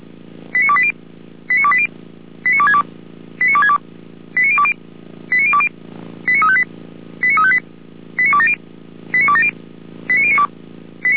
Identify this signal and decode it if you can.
Five-tone selective call (ZVEI), heard as quick runs of five tones, tones of about 70 ms. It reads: 89280 89280 89262 89262 89020 89020 89358 89358 89270 89270 89092